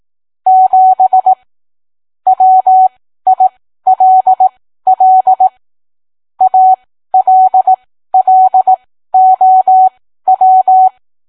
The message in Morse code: 7 WILL ALLOW